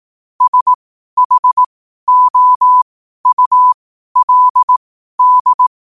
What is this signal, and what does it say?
Morse code, heard as a keyed tone, about 18 wpm, spaced as for 12 wpm, 1000 Hz. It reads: SHOULD
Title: 12 WPM morse code quotes for Fri, 01 Mar 2024